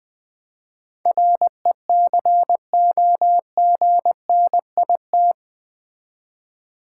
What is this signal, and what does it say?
Morse code, heard as a keyed tone, about 20 wpm, 700 Hz. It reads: RECOGNIT
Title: Build A Word - 4 Letters At A Time - 11-Letter Words - 20wpm